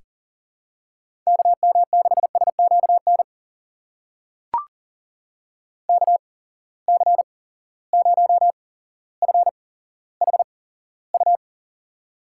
Morse code, heard as a keyed tone, about 39 wpm, spaced as for 17 wpm, 700 Hz. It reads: KM6SQN KC0FHU
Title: USA Call Signs - 6 Characters - 8x Character Spacing - 40wpm